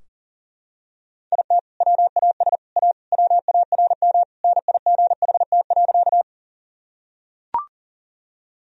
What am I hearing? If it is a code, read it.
Morse code, heard as a keyed tone, about 40 wpm, 700 Hz. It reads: IT WAS A WARM NIGHT.
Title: Sentences from Top 1000 Words - Review Part 2 - 40wpm